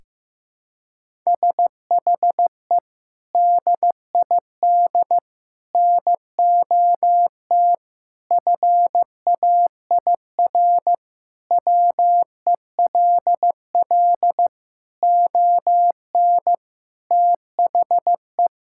Morse code, heard as a keyed tone, about 15 wpm, 700 Hz. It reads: SHE DID NOT FAIR WELL ON THE
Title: Sentences from Top 1000 Words - Review Part 5 - 15wpm